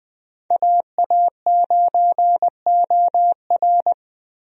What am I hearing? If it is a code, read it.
Morse code, heard as a keyed tone, about 20 wpm, 700 Hz. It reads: AA9OR
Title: USA Call Signs - 5 Characters - Repeated 3x - 20wpm